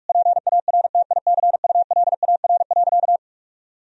Morse code, heard as a keyed tone, about 46 wpm, 700 Hz. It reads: PARTICULAR.
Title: Sentences from Top 1000 Words - Review Part 4 - 45wpm